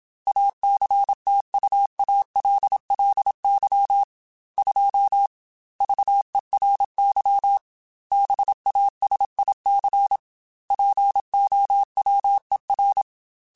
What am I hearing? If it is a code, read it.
Morse code, heard as a keyed tone, about 27 wpm, 800 Hz. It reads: ACTUALLY 2 VERY BASIC POWER
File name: Technical Subjects / Technical Subjects-mp3s / Sw Pwr Sup @25